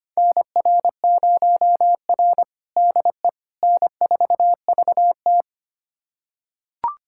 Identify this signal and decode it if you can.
Morse code, heard as a keyed tone, about 25 wpm, 700 Hz. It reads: NR0R DE N4VT
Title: USA Callsign DE Callsign - 4 Characters - 25wpm